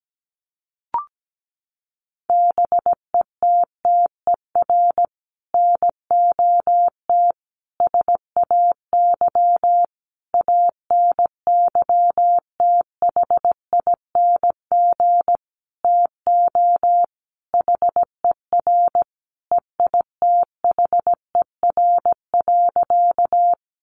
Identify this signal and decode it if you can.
Morse code, heard as a keyed tone, about 17 wpm, 700 Hz. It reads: BETTER NOT SAY ANYTHING TO HER EITHER.
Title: Sentences from Top 1000 Words 17wpm